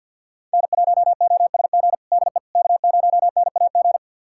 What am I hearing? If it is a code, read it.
Morse code, heard as a keyed tone, about 50 wpm, 700 Hz. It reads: N1OSG DE K0NAG
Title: USA Callsign DE Callsign - 5 Characters - 50wpm